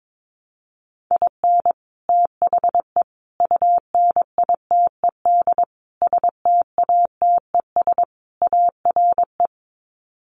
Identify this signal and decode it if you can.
Morse code, heard as a keyed tone, about 22 wpm, 700 Hz. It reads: IN THE UNITED STATES ARE
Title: Sets of 5 Words Encore 22wpm